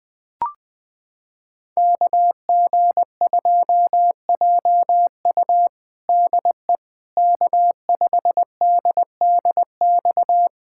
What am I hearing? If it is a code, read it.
Morse code, heard as a keyed tone, about 20 wpm, 700 Hz. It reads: KG2JU DE K5DDX
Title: USA Callsign DE Callsign - 5 Characters - 20wpm